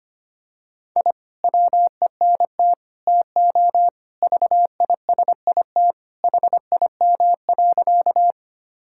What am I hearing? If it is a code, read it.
Morse code, heard as a keyed tone, about 25 wpm, 700 Hz. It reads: I WENT TO VISIT HIM.
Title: Sentences from Top 1000 Words - Review Part 6 - 25wpm